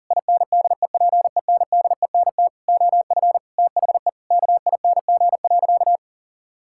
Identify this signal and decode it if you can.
Morse code, heard as a keyed tone, about 40 wpm, 700 Hz. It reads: INDEPENDENT OF THE KING.
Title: Sentences from Top 1000 Words - Review Part 6 - 40wpm